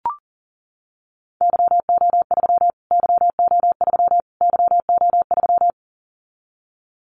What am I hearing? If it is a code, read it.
Morse code, heard as a keyed tone, about 38 wpm, 700 Hz. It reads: YO3 YO3 YO3
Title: Callsign Prefixes for Top 25 DXCC - Repeated 3x - 40wpm